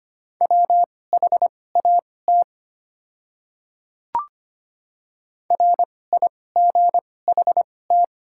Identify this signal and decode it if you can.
Morse code, heard as a keyed tone, about 25 wpm, spaced as for 18 wpm, 700 Hz. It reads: WHAT RIGHT
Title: 2x Character Spacing - Top 500 Words - 25wpm